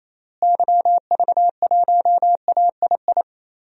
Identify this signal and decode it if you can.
Morse code, heard as a keyed tone, about 28 wpm, 700 Hz. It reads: YV1AII
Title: DXCC-100 Call Signs - 6 Characters - Repeated 2x - 28wpm